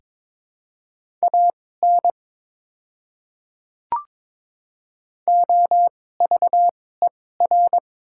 Morse code, heard as a keyed tone, about 22 wpm, spaced as for 16 wpm, 690 Hz. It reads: AN OVER
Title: 2x Character Spacing - Top 100 Words - 22wpm